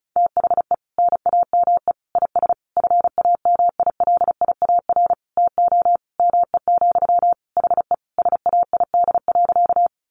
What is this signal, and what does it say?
Morse code, heard as a keyed tone, about 34 wpm, 700 Hz. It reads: THE NAME IS FAMILIAR TO ME, HE SAID.
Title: Sentences from Top 1000 Words 35wpm